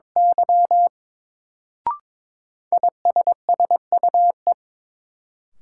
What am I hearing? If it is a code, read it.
Morse code, heard as a keyed tone, about 21 wpm, 700 Hz. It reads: Y ISSUE
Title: Sentences from Top 900 Words - New Word Drill - 22wpm